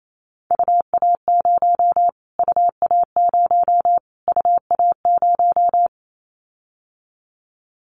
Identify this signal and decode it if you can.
Morse code, heard as a keyed tone, about 28 wpm, 700 Hz. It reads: UA0 UA0 UA0
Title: Callsign Prefixes for Top 25 DXCC - Repeated 3x - 28wpm